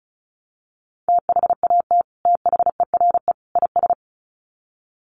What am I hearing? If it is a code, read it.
Morse code, heard as a keyed tone, about 34 wpm, 700 Hz. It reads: THAT THERE IS
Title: Sets of 3 Words 35wpm